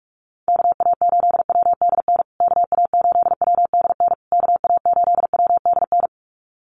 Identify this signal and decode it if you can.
Morse code, heard as a keyed tone, about 44 wpm, 700 Hz. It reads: KA8WDN KA8WDN KA8WDN
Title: USA Call Signs Repeated 3x 45wpm